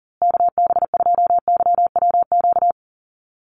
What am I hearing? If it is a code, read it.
Morse code, heard as a keyed tone, about 40 wpm, 700 Hz. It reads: KB2YWQ
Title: USA Call Signs Repeated 2x 40wpm